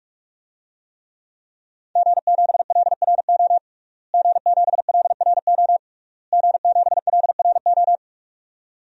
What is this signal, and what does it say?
Morse code, heard as a keyed tone, about 46 wpm, 700 Hz. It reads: G7LRO G7LRO G7LRO
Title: DXCC-100 Call Signs - 5 Characters - Repeated 3x - 45wpm